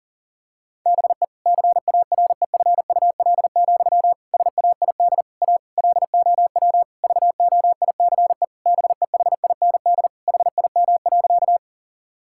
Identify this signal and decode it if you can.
Morse code, heard as a keyed tone, about 37 wpm, 700 Hz. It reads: BE CAREFUL, SAID A LOW VOICE BEHIND HIM.
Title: Sentences from Top 1600 Words 40wpm